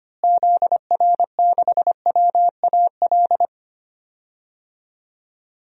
Morse code, heard as a keyed tone, about 25 wpm, 700 Hz. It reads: ZR6WAL
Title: DXCC-100 Call Signs - 6 Characters - Repeated 3x - 25wpm